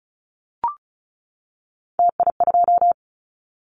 Morse code, heard as a keyed tone, about 34 wpm, 700 Hz. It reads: TI2
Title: Callsign Prefixes for Top 100 DXCC - 35wpm